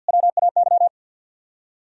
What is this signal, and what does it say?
Morse code, heard as a keyed tone, about 58 wpm, 700 Hz. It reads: WAY
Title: Speed-Racing - Sets of 3 Words 50wpm